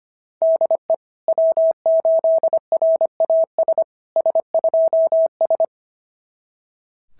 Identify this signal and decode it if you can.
Morse code, heard as a keyed tone, about 25 wpm, 690 Hz. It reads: DE W8RAS S2S
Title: SOTA QSO Practice 25wpm